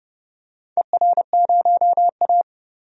Morse code, heard as a keyed tone, about 29 wpm, 700 Hz. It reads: ER0A
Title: DXCC-100 Call Signs - 4 Characters - 30wpm